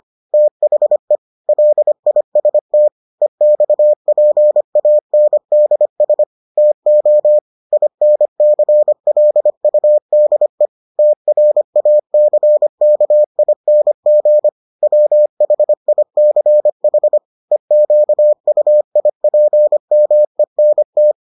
Morse code, heard as a keyed tone, about 25 wpm, 600 Hz. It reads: THE LIST EXPANDS TO INCLUDE TRACKING WHICH EQUIPMENT